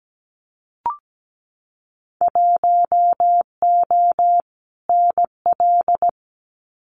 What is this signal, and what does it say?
Morse code, heard as a keyed tone, about 17 wpm, 700 Hz. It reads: 1O NL
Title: Winter Field Day Exchanges Encore 17wpm